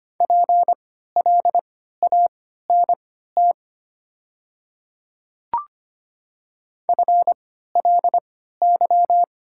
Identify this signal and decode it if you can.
Morse code, heard as a keyed tone, about 25 wpm, spaced as for 14 wpm, 700 Hz. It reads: PLANT FLY
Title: 3x Character Spacing - Top 500 Words - 25wpm